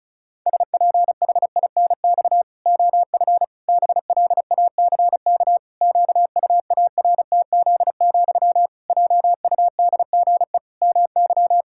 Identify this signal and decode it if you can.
Morse code, heard as a keyed tone, about 34 wpm, 700 Hz. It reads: SPHINX OF BLACK QUARTZ, JUDGE MY
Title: Pangram Challenge v2 - 35wpm